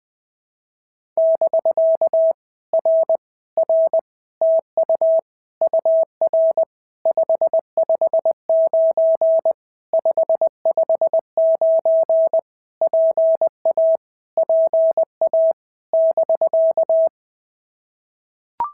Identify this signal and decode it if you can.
Morse code, heard as a keyed tone, about 20 wpm, 660 Hz. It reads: <BK> R R TU UR 559 559 PA PA <BK>
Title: POTA QSO Practice - 20wpm